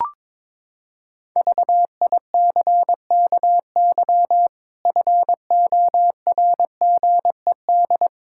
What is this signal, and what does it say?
Morse code, heard as a keyed tone, about 22 wpm, 700 Hz. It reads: VICKY FORGED
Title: Pangram Challenge v2 - 22wpm